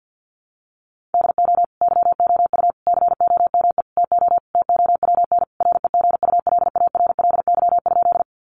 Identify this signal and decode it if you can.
Morse code, heard as a keyed tone, about 50 wpm, 700 Hz. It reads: DO YOU COME TO TOWN REGULARLY?
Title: Sentences from Top 1400 Words 50wpm